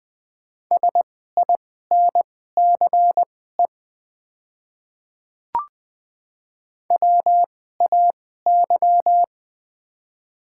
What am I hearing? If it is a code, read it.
Morse code, heard as a keyed tone, about 20 wpm, spaced as for 14 wpm, 700 Hz. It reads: SINCE WAY
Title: 2x Character Spacing - Top 300 Words - 20wpm